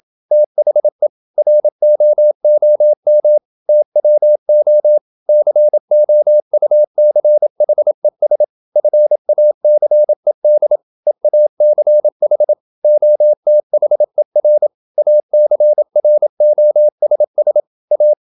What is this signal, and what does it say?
Morse code, heard as a keyed tone, about 27 wpm, 600 Hz. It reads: THE ROOM TWO COUCHES FACED EACH OTHER ACROSS A
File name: Never 0027wpm